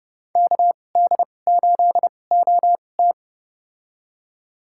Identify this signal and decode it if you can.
Morse code, heard as a keyed tone, about 30 wpm, spaced as for 22 wpm, 700 Hz. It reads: KD8OT
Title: USA Call Signs - 5 Characters - 2x Character Spacing - 30wpm